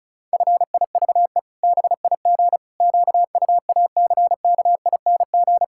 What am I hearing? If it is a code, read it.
Morse code, heard as a keyed tone, about 34 wpm, 700 Hz. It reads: FIVE BIG QUACKING
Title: Pangram Challenge v2 - 35wpm